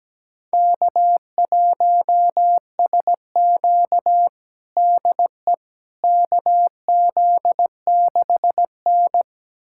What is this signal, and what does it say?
Morse code, heard as a keyed tone, about 17 wpm, 700 Hz. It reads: K1SQ DE KZ6N
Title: USA Callsign DE Callsign - 4 Characters - 17wpm